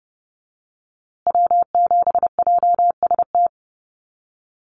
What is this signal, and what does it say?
Morse code, heard as a keyed tone, about 30 wpm, 700 Hz. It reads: W7JST
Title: USA Call Signs Repeated 2x 30wpm